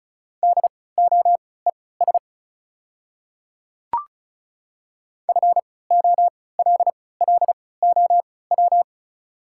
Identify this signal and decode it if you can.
Morse code, heard as a keyed tone, about 34 wpm, spaced as for 20 wpm, 700 Hz. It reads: DOES FOLLOW
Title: 3x Character Spacing - Top 200 Words - 35wpm